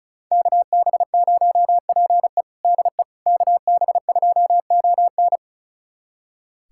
Morse code, heard as a keyed tone, about 36 wpm, 700 Hz. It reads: KB0PE DE KB2ON
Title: USA Callsign DE Callsign - 5 Characters - 35wpm